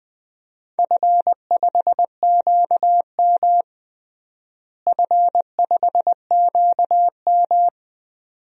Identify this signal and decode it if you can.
Morse code, heard as a keyed tone, about 21 wpm, 700 Hz. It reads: F5QM F5QM
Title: DXCC-100 Call Signs - 4 Characters - Repeated 2x - 20wpm